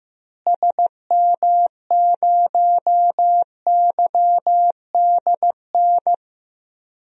Morse code, heard as a keyed tone, about 16 wpm, 700 Hz. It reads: SM0YDN
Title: DXCC-100 Call Signs - 6 Characters - Repeated 2x - 15wpm